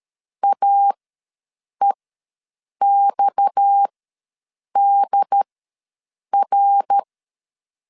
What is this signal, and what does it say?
Morse code, heard as a keyed tone, about 13 wpm, spaced as for 7 wpm, 800 Hz. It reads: AEXDR
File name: LEZIONI / 9 2 ws 12wpm 3406